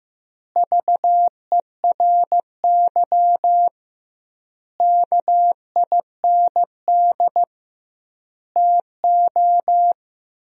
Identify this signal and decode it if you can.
Morse code, heard as a keyed tone, about 15 wpm, 700 Hz. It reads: VERY KIND TO
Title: 2x Word-Spacing - Sentences from Top 200 Words - 15wpm